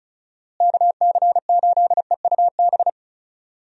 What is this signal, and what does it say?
Morse code, heard as a keyed tone, about 36 wpm, 700 Hz. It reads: KC8EUB
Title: USA Call Signs - 6 Characters - Repeated 2x - 35wpm